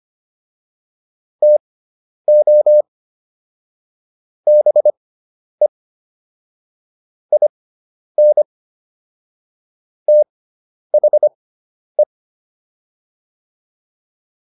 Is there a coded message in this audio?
Yes: Morse code from 1.4 to 12.0 s, heard as a keyed tone, about 25 wpm, spaced as for 10 wpm, 600 Hz. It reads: TO BE IN THE